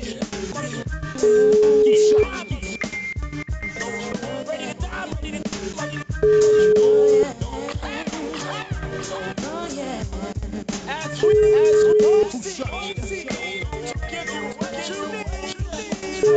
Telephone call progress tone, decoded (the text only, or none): ringback tone